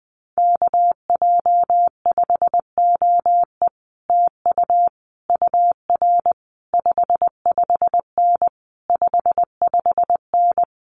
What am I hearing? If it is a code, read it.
Morse code, heard as a keyed tone, about 20 wpm, 700 Hz. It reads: KJ5OE TU UR 55N 55N
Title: Common QSO Phrases Encore 20wpm